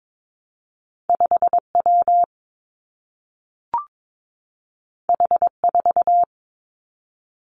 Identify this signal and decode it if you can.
Morse code, heard as a keyed tone, about 22 wpm, 700 Hz. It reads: 5 W H 4
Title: Two Challenging Letter-Number Combinations 22wpm